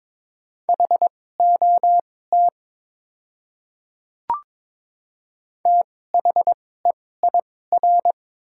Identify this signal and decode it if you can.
Morse code, heard as a keyed tone, about 22 wpm, spaced as for 16 wpm, 700 Hz. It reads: HOT THEIR
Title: 2x Character Spacing - Top 300 Words - 22wpm